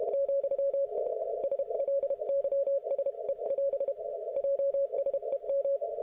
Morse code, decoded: , SLYSE <AS> JSEM

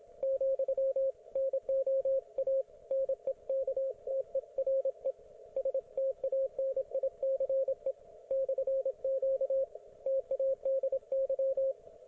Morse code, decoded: , NOA NEKTERE STANICE /Q TADY

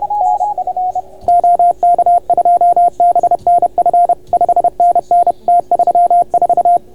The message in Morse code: NF OK2BNF 5NN T34

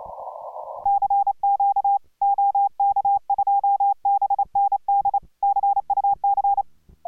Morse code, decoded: CQ OK2BND CUC

